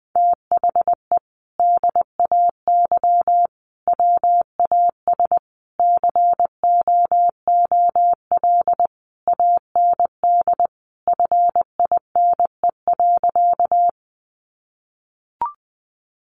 Morse code, heard as a keyed tone, about 20 wpm, 700 Hz. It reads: THE DAY WAS COOL AND FINE.
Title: Sentences from Top 1200 Words 20wpm